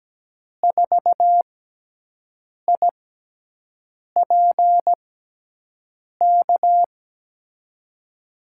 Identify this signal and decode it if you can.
Morse code, heard as a keyed tone, about 17 wpm, spaced as for 6 wpm, 700 Hz. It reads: 4IPK
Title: USA Call Signs - 6x Character Spacing - 17wpm